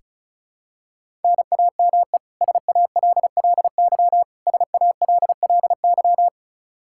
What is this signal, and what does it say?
Morse code, heard as a keyed tone, about 36 wpm, 700 Hz. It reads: NAME SALLY SALLY
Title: K1USN SST QSO Practice 35wpm